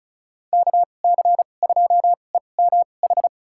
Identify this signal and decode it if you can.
Morse code, heard as a keyed tone, about 36 wpm, spaced as for 25 wpm, 700 Hz. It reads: KC2EMH
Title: USA Call Signs - 6 Characters - 2x Character Spacing - 35wpm